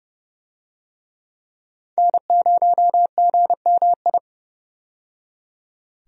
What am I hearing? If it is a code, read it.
Morse code, heard as a keyed tone, about 30 wpm, 700 Hz. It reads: N0GMI
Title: USA Call Signs - 5 Characters - 30wpm